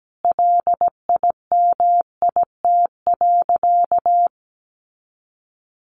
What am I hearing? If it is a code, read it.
Morse code, heard as a keyed tone, about 17 wpm, 700 Hz. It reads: LIMIT.
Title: Sentences from Top 1500 Words 17wpm